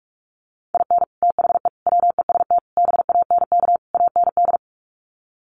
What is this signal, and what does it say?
Morse code, heard as a keyed tone, about 44 wpm, 700 Hz. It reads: IN THE WEST BANK AND
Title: Sets of 5 Words Encore2 45wpm